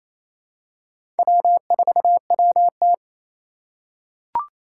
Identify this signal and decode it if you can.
Morse code, heard as a keyed tone, about 27 wpm, 700 Hz. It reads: W4WT
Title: USA Call Signs - 3x Character Spacing - 28wpm